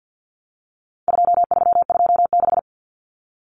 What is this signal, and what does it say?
Morse code, heard as a keyed tone, about 50 wpm, 700 Hz. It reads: 2326
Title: Common Cut Numbers - 4-Digit 50wpm